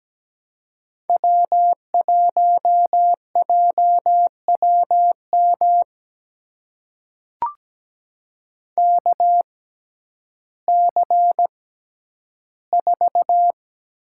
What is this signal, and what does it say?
Morse code, heard as a keyed tone, about 17 wpm, spaced as for 10 wpm, 700 Hz. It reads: W1JWM KC4
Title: USA Call Signs - 5 Characters - 6x Character Spacing - 17wpm